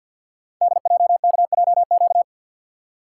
Morse code, heard as a keyed tone, about 50 wpm, 690 Hz. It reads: DJKJQ
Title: Code Groups - 5 Letters - 50wpm